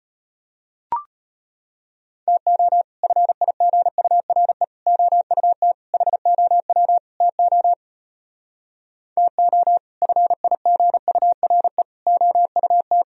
Morse code, heard as a keyed tone, about 33 wpm, 700 Hz. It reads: TO FIGURE OUT HOW TO TO FIGURE OUT